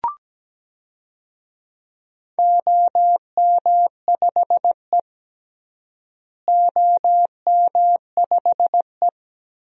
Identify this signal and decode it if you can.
Morse code, heard as a keyed tone, about 17 wpm, 700 Hz. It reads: OM5E OM5E
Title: DXCC-100 Call Signs - 4 Characters - Repeated 3x - 17wpm